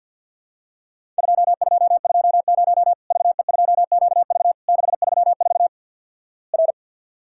Morse code, heard as a keyed tone, about 53 wpm, 700 Hz. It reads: 2220 VE2QV 634 R
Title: Worked All Europe DX Contest - QTC Exchange with Repeats - 50wpm